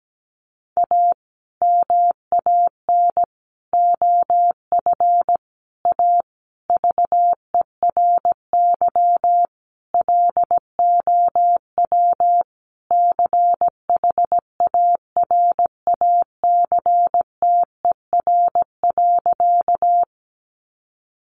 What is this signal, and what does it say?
Morse code, heard as a keyed tone, about 17 wpm, 700 Hz. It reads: A MAN OF A VERY LOW CHARACTER.
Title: Sentences from Top 1200 Words 17wpm